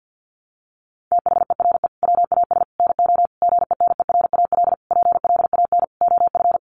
Morse code, heard as a keyed tone, about 50 wpm, 700 Hz. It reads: THERE WAS NO GENERAL PLAN OF
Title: Sentences from Top 1200 Words 50wpm